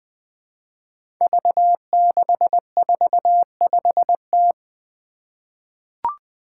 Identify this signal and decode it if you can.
Morse code, heard as a keyed tone, about 20 wpm, 700 Hz. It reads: V645T
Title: Five Challenging Letter-Number Combinations 20wpm